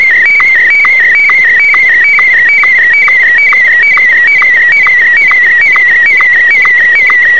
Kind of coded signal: SSTV (slow-scan television)